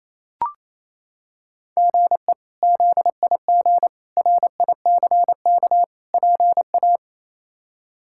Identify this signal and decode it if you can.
Morse code, heard as a keyed tone, about 28 wpm, 700 Hz. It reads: GE ZIG RICK PA
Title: K1USN SST QSO Practice - 28wpm